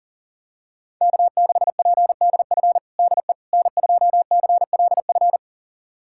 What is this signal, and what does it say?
Morse code, heard as a keyed tone, about 40 wpm, 700 Hz. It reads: K6PDF DE N2CLF
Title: USA Callsign DE Callsign - 5 Characters - 40wpm